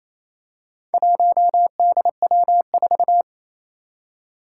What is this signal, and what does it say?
Morse code, heard as a keyed tone, about 28 wpm, 700 Hz. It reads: 1DW4
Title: Four Challenging Letter-Number Combinations 28wpm